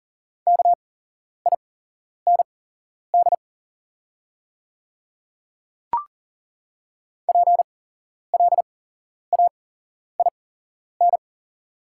Morse code, heard as a keyed tone, about 38 wpm, spaced as for 11 wpm, 690 Hz. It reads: KIND PLAIN
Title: 8x Character Spacing - Top 500 Words - 40wpm